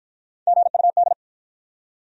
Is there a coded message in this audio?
Yes: Morse code from 0.5 to 1.1 s, heard as a keyed tone, about 53 wpm, 700 Hz. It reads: GUD